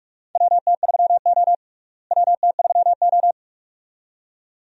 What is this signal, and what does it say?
Morse code, heard as a keyed tone, about 43 wpm, 700 Hz. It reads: WT3O WT3O
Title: USA Call Signs - 4 Characters - Repeated 2x - 45wpm